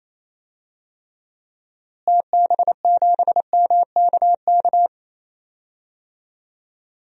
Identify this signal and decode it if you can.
Morse code, heard as a keyed tone, about 28 wpm, 700 Hz. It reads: TB7MKK
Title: DXCC-100 Call Signs - 6 Characters - Repeated 2x - 28wpm